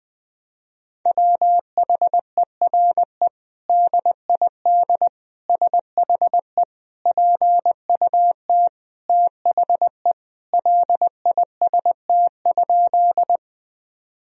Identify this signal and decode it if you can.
Morse code, heard as a keyed tone, about 20 wpm, 700 Hz. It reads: WHERE DID SHE PUT THE LIST?
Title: Sentences from Top 1000 Words - Review Part 2 - 20wpm